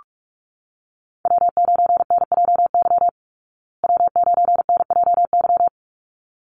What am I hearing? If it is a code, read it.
Morse code, heard as a keyed tone, about 44 wpm, 700 Hz. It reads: W9NJY W9NJY
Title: Call Signs Repeated 3x - 45wpm